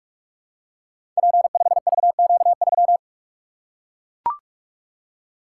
Morse code, heard as a keyed tone, about 44 wpm, 700 Hz. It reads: P5VQ3
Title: Code Groups - 5 Letter:Numbers - 45wpm